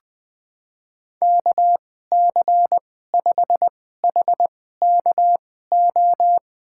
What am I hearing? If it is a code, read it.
Morse code, heard as a keyed tone, about 21 wpm, spaced as for 15 wpm, 700 Hz. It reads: KC5HKO
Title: USA Call Signs - 6 Characters - 2x Character Spacing - 20wpm